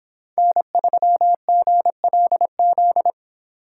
Morse code, heard as a keyed tone, about 26 wpm, 700 Hz. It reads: N3GLZ